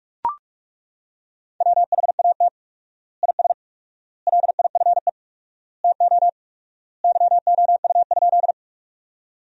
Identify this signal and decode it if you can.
Morse code, heard as a keyed tone, about 42 wpm, 700 Hz. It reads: WHAT IS LIFE TO YOU?
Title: 4x Word-Spacing - Sentences from Top 300 Words - 45wpm